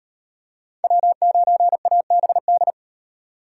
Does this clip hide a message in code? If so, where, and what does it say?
Yes: Morse code from 0.8 to 2.7 s, heard as a keyed tone, about 36 wpm, 700 Hz. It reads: W9ABD